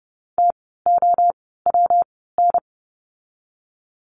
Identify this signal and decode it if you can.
Morse code, heard as a keyed tone, about 30 wpm, spaced as for 17 wpm, 700 Hz. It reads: TOWN
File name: course / 30:10wpm / 115 Lesson 19f - Letter w - Words 30wpm